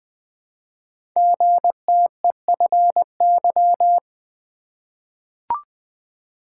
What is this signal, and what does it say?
Morse code, heard as a keyed tone, about 20 wpm, 700 Hz. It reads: GTEFY
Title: Code Groups - 5 Letter:Numbers - 20wpm